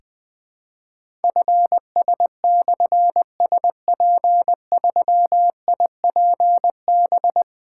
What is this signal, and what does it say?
Morse code, heard as a keyed tone, about 20 wpm, 700 Hz. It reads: FS/SP3IPB
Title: Difficult Call Signs 20wpm